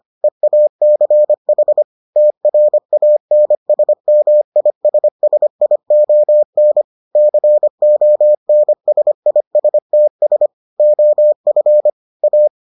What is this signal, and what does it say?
Morse code, heard as a keyed tone, about 25 wpm, 600 Hz. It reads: EACH TRANSMISSION CONSISTS OF A